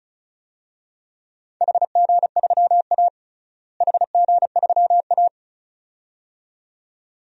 Morse code, heard as a keyed tone, about 34 wpm, 700 Hz. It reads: HG3A HG3A
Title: DXCC-100 Call Signs - 4 Characters - Repeated 2x - 35wpm